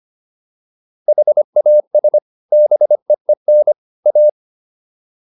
Morse code, heard as a keyed tone, about 25 wpm, 600 Hz. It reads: HAS BEEN A